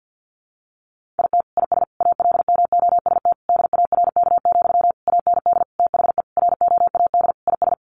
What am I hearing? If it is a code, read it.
Morse code, heard as a keyed tone, about 50 wpm, 700 Hz. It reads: IT IS ALMOST DARK, AND THE ROAD IS
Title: Sentences from Top 1200 Words 50wpm